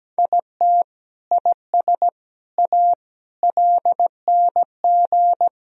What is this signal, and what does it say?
Morse code, heard as a keyed tone, about 17 wpm, 700 Hz. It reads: IT IS A LNG
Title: Sentences from Top 300 Words and Abbreviations 17wpm